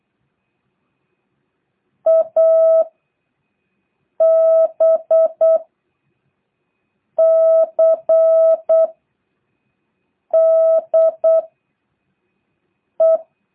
Morse code, heard as a keyed tone, about 8 wpm, 650 Hz. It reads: A B C D E
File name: jh8cbh / 2023cw